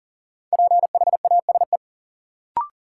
Morse code, heard as a keyed tone, about 38 wpm, 700 Hz. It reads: PHASE
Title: Sentences from Top 1200 Words - New Word Drill Alt - 40wpm